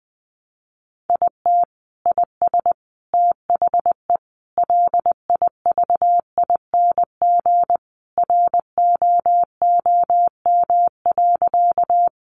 Morse code, heard as a keyed tone, about 20 wpm, 700 Hz. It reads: IT IS THE LIVING ROOM.
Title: Sentences from Top 1100 Words 20wpm